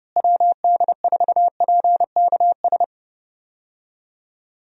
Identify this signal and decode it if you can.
Morse code, heard as a keyed tone, about 32 wpm, 700 Hz. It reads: WD4PKS
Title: USA Call Signs - 6 Characters - Repeated 3x - 30wpm